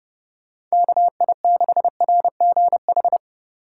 Morse code, heard as a keyed tone, about 32 wpm, 700 Hz. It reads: KI6RGH